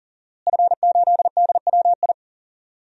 Morse code, heard as a keyed tone, about 40 wpm, 700 Hz. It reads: F8DWI
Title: DXCC-100 Call Signs - 5 Characters - Repeated 3x - 40wpm